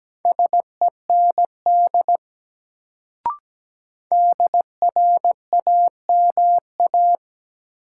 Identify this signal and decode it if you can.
Morse code, heard as a keyed tone, about 17 wpm, 700 Hz. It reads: SEND DRAMA
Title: Sentences from Top 1500 Words - New Word Drill Alt - 17wpm